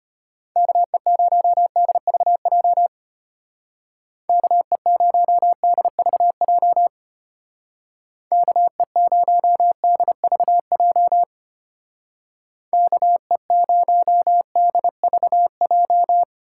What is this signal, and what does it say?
Morse code, heard as a keyed tone, about 32 wpm, 700 Hz. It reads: KE0DVJ KE0DVJ KE0DVJ KE0DVJ